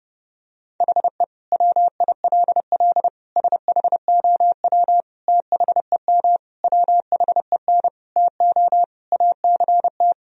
Morse code, heard as a keyed tone, about 30 wpm, 700 Hz. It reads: HE WILL SHOW THEM WHEN TO ACT
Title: Sentences from Top 1000 Words - Review Part 1 - 30wpm